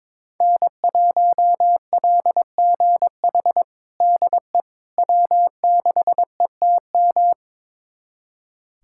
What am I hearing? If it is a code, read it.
Morse code, heard as a keyed tone, about 21 wpm, 700 Hz. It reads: N1LGH DE W6ETM
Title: USA Callsign DE Callsign - 5 Characters - 22wpm